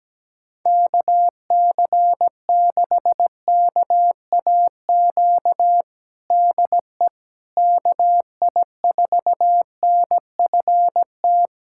Morse code, heard as a keyed tone, about 17 wpm, 700 Hz. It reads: KC6KAQ DE KI4NFT
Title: USA Callsign DE Callsign - 6 Characters - 17wpm